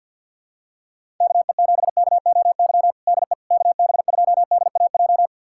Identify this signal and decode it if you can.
Morse code, heard as a keyed tone, about 48 wpm, 700 Hz. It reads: KE7KOY DE KB2DAJ
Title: USA Callsign DE Callsign - 6 Characters - 50wpm